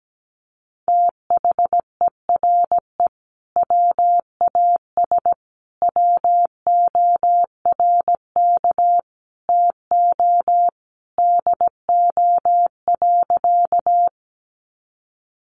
Morse code, heard as a keyed tone, about 17 wpm, 700 Hz. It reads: THERE WAS WORK TO DO.
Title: Sentences from Top 200 Words 17wpm